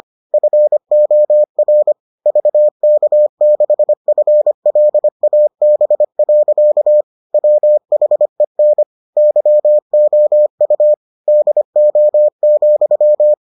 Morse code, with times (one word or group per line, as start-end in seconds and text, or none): 0.3-1.9 FOR
2.3-7.0 VK6FLAB.
7.3-8.8 WHEN
9.2-10.9 YOU
11.3-13.3 DO,